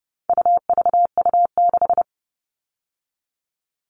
Morse code, heard as a keyed tone, about 30 wpm, 700 Hz. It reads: UVU6
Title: All Cut Numbers - 4-Digit 30wpm